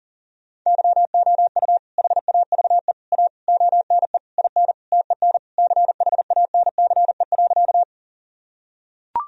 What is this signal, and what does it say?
Morse code, heard as a keyed tone, about 40 wpm, 700 Hz. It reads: YOU HAVE A ONE IN TEN CHANCE.
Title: Sentences from Top 1000 Words - Review Part 2 - 40wpm